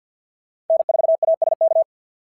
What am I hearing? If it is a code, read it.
Morse code, heard as a keyed tone, about 51 wpm, 690 Hz. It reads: N4ASK
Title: SOTA QSO Practice 50wpm